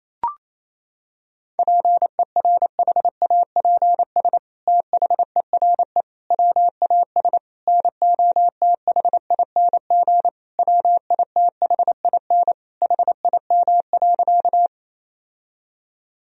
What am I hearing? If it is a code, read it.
Morse code, heard as a keyed tone, about 28 wpm, 700 Hz. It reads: PERHAPS THERE WAS NOTHING WITHIN HIM.
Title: Sentences from Top 1000 Words - Review Part 4 - 28wpm